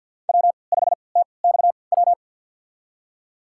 Morse code, heard as a keyed tone, about 51 wpm, spaced as for 29 wpm, 700 Hz. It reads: W5TXW
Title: USA Call Signs - 3x Character Spacing - 50wpm